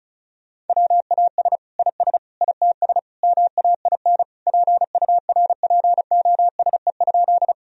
Morse code, heard as a keyed tone, about 34 wpm, 700 Hz. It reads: WAS IS ITS MAIN PURPOSE?
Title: Sentences from Top 1000 Words - Review Part 2 - 35wpm